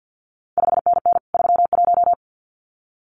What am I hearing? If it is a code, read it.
Morse code, heard as a keyed tone, about 50 wpm, 700 Hz. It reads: 5NN 31
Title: CQ WW DX Contest Exchange - 50wpm